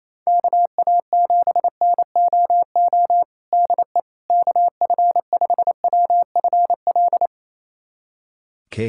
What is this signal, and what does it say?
Morse code, heard as a keyed tone, about 28 wpm, 700 Hz. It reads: KA7NOO DE KF5WFL
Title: USA Callsign DE Callsign 28wpm